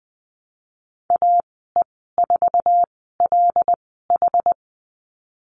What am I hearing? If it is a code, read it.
Morse code, heard as a keyed tone, about 19 wpm, spaced as for 14 wpm, 700 Hz. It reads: AE4LH
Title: 173 Lesson 28g - Number 9 - Call Signs 20wpm